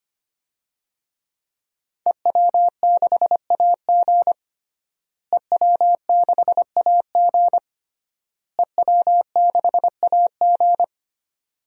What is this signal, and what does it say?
Morse code, heard as a keyed tone, about 25 wpm, 700 Hz. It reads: EW6AG EW6AG EW6AG
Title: DXCC-100 Call Signs - 5 Characters - Repeated 3x - 25wpm